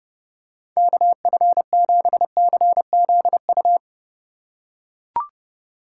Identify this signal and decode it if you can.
Morse code, heard as a keyed tone, about 30 wpm, 700 Hz. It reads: KF7CZU